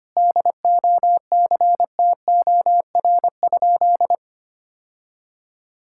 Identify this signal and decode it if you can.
Morse code, heard as a keyed tone, about 25 wpm, 700 Hz. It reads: DOCTOR?
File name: practice / 25wpm / Sentences with 4-Word Phrases - 25wpm